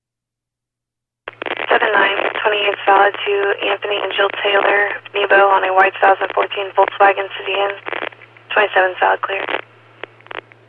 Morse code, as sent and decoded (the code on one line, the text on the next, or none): none